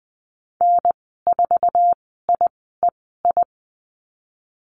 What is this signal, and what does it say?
Morse code, heard as a keyed tone, about 20 wpm, spaced as for 14 wpm, 700 Hz. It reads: N4IEI
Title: 082 Lesson 14g - Number 2 - Call Signs 20wpm